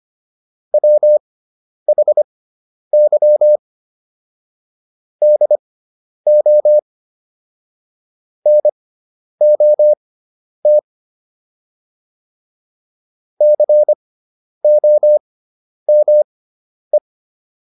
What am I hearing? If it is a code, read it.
Morse code, heard as a keyed tone, about 25 wpm, spaced as for 10 wpm, 600 Hz. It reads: WHY DO NOT COME